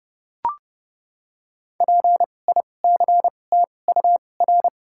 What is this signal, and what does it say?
Morse code, heard as a keyed tone, about 30 wpm, spaced as for 22 wpm, 700 Hz. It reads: PICTUR
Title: 2x Character Spacing - Top 200 Words - 30wpm